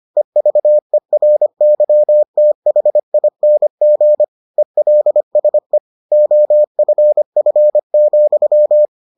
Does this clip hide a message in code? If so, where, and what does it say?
Yes: Morse code from 0.2 to 8.8 s, heard as a keyed tone, about 25 wpm, 600 Hz. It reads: EVERYTHING ELSE OFF,